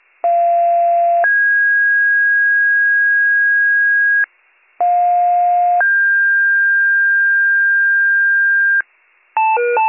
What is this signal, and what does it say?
Two-tone page, heard as a one-second tone, then a three-second tone, about 690 then 1650 Hz, about 710 then 1600 Hz. It is heard from 0.2 to 8.8 s.